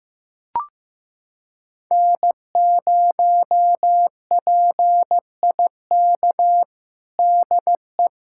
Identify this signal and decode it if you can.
Morse code, heard as a keyed tone, about 15 wpm, 700 Hz. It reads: N0PIK DE
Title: USA Callsign DE Callsign 15wpm